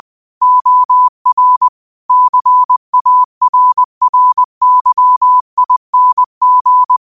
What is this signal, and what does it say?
Morse code, heard as a keyed tone, about 20 wpm, 1000 Hz. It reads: OR CARRYING
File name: qotdfiles / 20wpm